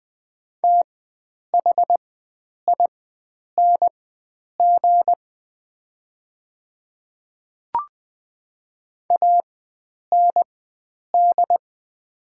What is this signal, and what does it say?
Morse code, heard as a keyed tone, about 20 wpm, spaced as for 9 wpm, 700 Hz. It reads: THING AND
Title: 4x Character Spacing - Top 100 Words - 20wpm